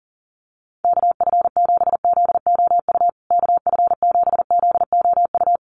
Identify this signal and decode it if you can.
Morse code, heard as a keyed tone, about 37 wpm, 700 Hz. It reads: KF7ZOU KF7ZOU
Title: USA Call Signs Repeated 3x 40wpm